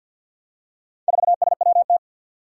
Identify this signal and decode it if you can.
Morse code, heard as a keyed tone, about 53 wpm, 700 Hz. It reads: 4SWT
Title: Four Challenging Letter-Number Combinations 50wpm